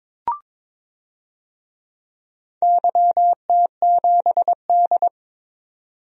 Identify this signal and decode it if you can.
Morse code, heard as a keyed tone, about 22 wpm, 700 Hz. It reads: YT7D
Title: DXCC-100 Call Signs - 4 Characters - 22wpm